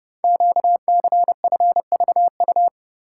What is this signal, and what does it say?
Morse code, heard as a keyed tone, about 30 wpm, 700 Hz. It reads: QCFVU
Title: Code Groups - 5 Letters - 30wpm